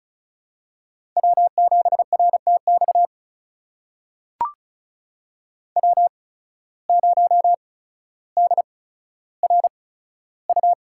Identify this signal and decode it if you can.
Morse code, heard as a keyed tone, about 36 wpm, spaced as for 15 wpm, 690 Hz. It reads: W7RTX W0DRU
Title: USA Call Signs - 5 Characters - 8x Character Spacing - 35wpm